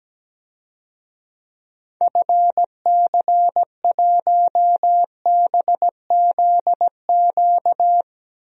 Morse code, heard as a keyed tone, about 17 wpm, 700 Hz. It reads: FC1BZQ
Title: DXCC-100 Call Signs - 6 Characters - Repeated 2x - 17wpm